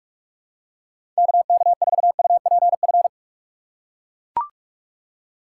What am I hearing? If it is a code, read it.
Morse code, heard as a keyed tone, about 42 wpm, 700 Hz. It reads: KK4UPF
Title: USA Call Signs - 2x Character Spacing - 45wpm